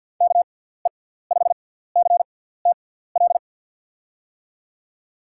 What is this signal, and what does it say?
Morse code, heard as a keyed tone, about 51 wpm, spaced as for 17 wpm, 700 Hz. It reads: KE5CTL
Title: USA Call Signs - 6 Characters - 6x Character Spacing - 50wpm